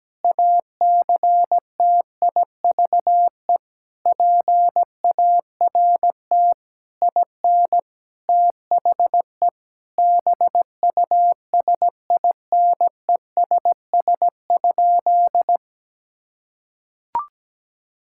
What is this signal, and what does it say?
Morse code, heard as a keyed tone, about 17 wpm, 700 Hz. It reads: ACTIVE PART IN THE BUSINESS?
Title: Sentences from Top 1000 Words - Review Part 6 - 17wpm